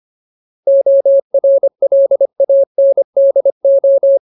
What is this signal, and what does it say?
Morse code, heard as a keyed tone, about 25 wpm, 550 Hz. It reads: ORLANDO
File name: Ultimate Word List / 25WPM Ultimate